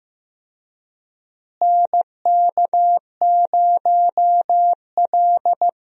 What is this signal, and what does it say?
Morse code, heard as a keyed tone, about 15 wpm, 700 Hz. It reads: NK0L
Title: USA Call Signs - 4 Characters - Repeated 3x - 15wpm